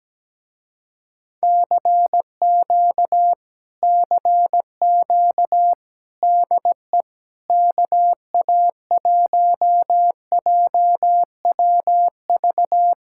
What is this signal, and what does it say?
Morse code, heard as a keyed tone, about 17 wpm, 700 Hz. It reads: CQ CQ DE KA1JWV
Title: Speed-Racing Common QSO Phrases Encore - 17wpm